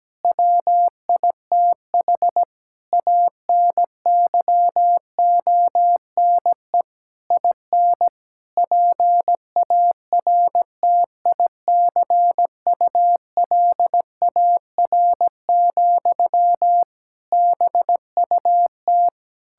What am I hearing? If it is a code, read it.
Morse code, heard as a keyed tone, about 17 wpm, 700 Hz. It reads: WITH ANYONE IN PARTICULAR, BUT
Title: Sentences from Top 1000 Words - Review Part 5 - 17wpm